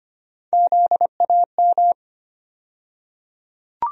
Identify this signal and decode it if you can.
Morse code, heard as a keyed tone, about 25 wpm, 700 Hz. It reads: ZAM
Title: USA Call Sign Postfixes - 3 Letters 25wpm